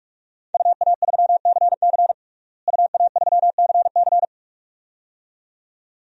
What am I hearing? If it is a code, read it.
Morse code, heard as a keyed tone, about 42 wpm, 700 Hz. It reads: UA3CC UA3CC
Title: DXCC-100 Call Signs - 5 Characters - Repeated 2x - 45wpm